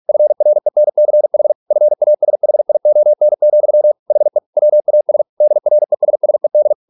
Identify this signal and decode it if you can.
Morse code, heard as a keyed tone, about 46 wpm, 600 Hz. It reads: FRENCH FASHION, HE WAS DRESSED